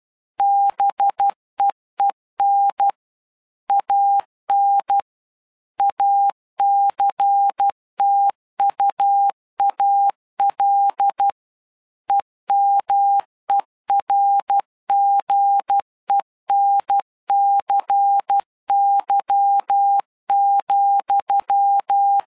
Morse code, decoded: BEEN AN ACTUAL EMERGENCY,